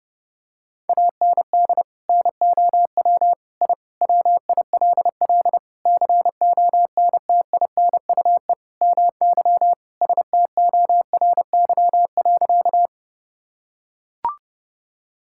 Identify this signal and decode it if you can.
Morse code, heard as a keyed tone, about 32 wpm, 700 Hz. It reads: AND NOW I WILL CONTINUE MY STORY.
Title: Sentences from Top 1000 Words - Review Part 6 - 30wpm